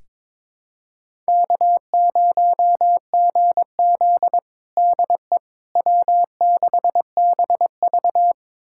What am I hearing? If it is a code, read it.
Morse code, heard as a keyed tone, about 22 wpm, 700 Hz. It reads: K0GZ DE W6BV
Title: USA Callsign DE Callsign - 4 Characters - 22wpm